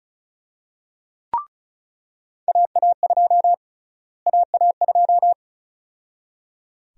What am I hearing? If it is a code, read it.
Morse code, heard as a keyed tone, about 34 wpm, 700 Hz. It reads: AA2 AA2
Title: USA Call Sign Prefixes - Repeated 2x - 35wpm